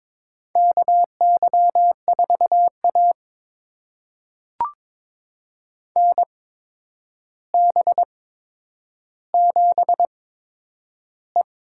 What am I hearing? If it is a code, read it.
Morse code, heard as a keyed tone, about 22 wpm, spaced as for 9 wpm, 700 Hz. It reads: KY4A NB7E
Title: USA Call Signs - 4 Characters - 8x Character Spacing - 22wpm